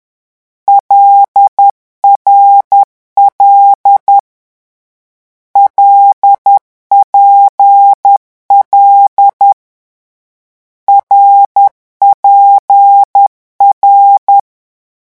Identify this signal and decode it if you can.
Morse code, heard as a keyed tone, about 10 wpm, 800 Hz. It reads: LRL LPL RPR
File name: morse mp3 / lecon6-vitesse 10